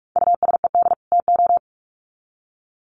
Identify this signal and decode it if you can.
Morse code, heard as a keyed tone, about 44 wpm, 700 Hz. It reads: USED TO
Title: Sets of 2 Words 45wpm